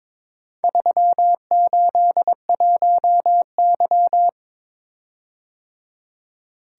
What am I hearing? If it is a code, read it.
Morse code, heard as a keyed tone, about 21 wpm, 700 Hz. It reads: 381Y
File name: practice / 22wpm / Four Letter-Number Combinations - 22wpm